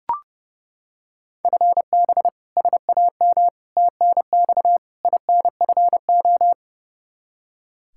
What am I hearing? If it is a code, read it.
Morse code, heard as a keyed tone, about 30 wpm, 700 Hz. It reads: FB SAM TNX INFO
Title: Practice QSO - Exchange 3 - More Info - 30wpm